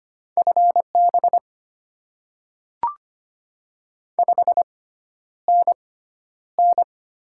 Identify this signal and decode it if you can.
Morse code, heard as a keyed tone, about 25 wpm, spaced as for 11 wpm, 700 Hz. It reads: FB 5NN
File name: Top 20 QSO Elements - 6x Character Spacing - 25wpm